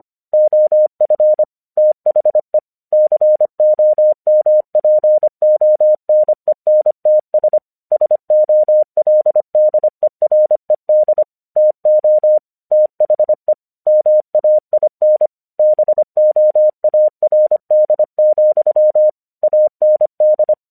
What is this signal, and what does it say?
Morse code, heard as a keyed tone, about 25 wpm, 610 Hz. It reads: OF THE COMPONENTS SOLDERED TO THE MAIN BOARD, AND
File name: 1 English 25wpm / AQC24